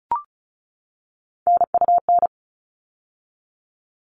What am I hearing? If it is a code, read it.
Morse code, heard as a keyed tone, about 34 wpm, 700 Hz. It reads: NUN
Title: All Cut Numbers - 3-Digit 35wpm